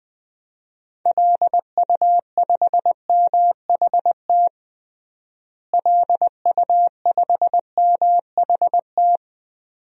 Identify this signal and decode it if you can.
Morse code, heard as a keyed tone, about 21 wpm, 700 Hz. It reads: LU5MHT LU5MHT
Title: DXCC-100 Call Signs - 6 Characters - Repeated 2x - 20wpm